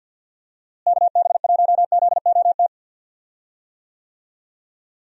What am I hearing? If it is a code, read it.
Morse code, heard as a keyed tone, about 49 wpm, 700 Hz. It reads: KB1ZOT
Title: USA Call Signs - 6 Characters - Repeated 3x - 50wpm